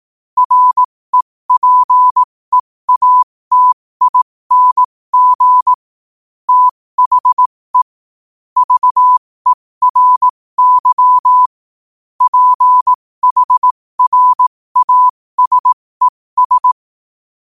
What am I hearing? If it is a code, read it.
Morse code, heard as a keyed tone, about 18 wpm, spaced as for 15 wpm, 1000 Hz. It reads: REPEATING THE VERY PHRASES
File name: qotdfiles / 15wpm